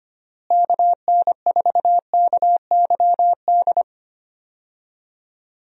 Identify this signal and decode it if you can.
Morse code, heard as a keyed tone, about 25 wpm, 700 Hz. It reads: KN4KYD
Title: USA Call Signs - 6 Characters - Repeated 2x - 25wpm